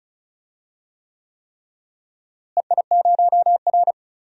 Morse code, heard as a keyed tone, about 34 wpm, 700 Hz. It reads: EI0R
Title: DXCC-100 Call Signs - 4 Characters - Repeated 3x - 35wpm